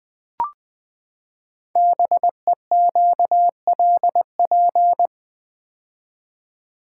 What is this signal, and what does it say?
Morse code, heard as a keyed tone, about 20 wpm, 700 Hz. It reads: BEQLP
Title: Code Groups - 5 Letters - 20wpm